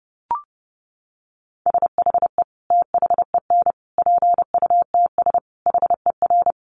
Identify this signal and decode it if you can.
Morse code, heard as a keyed tone, about 30 wpm, 700 Hz. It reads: SHE THEN PUTS HER
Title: Easy Phrases 30wpm